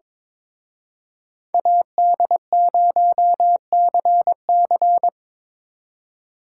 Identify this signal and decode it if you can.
Morse code, heard as a keyed tone, about 26 wpm, 700 Hz. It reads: AD0CC
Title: Speed Racing - USA Call Signs 22wpm